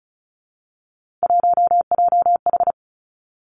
Morse code, heard as a keyed tone, about 36 wpm, 700 Hz. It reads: 1JH
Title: Three Challenging Letter-Number Combinations 35wpm